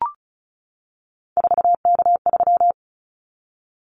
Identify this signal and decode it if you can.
Morse code, heard as a keyed tone, about 34 wpm, 700 Hz. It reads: 4K3
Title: Callsign Prefixes for Top 100 DXCC - 35wpm